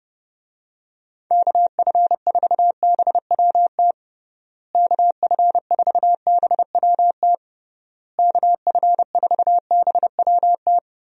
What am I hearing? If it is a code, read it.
Morse code, heard as a keyed tone, about 32 wpm, 700 Hz. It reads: KF4BWT KF4BWT KF4BWT
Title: USA Call Signs - 6 Characters - Repeated 3x - 30wpm